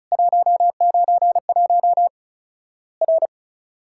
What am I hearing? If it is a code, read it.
Morse code, heard as a keyed tone, about 36 wpm, 700 Hz. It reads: 191 R
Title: Worked All Europe DX Contest - QTC Exchange - 35wpm